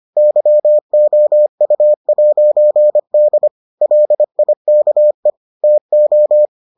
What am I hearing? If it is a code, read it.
Morse code, heard as a keyed tone, about 25 wpm, 600 Hz. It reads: YOU'D LIKE TO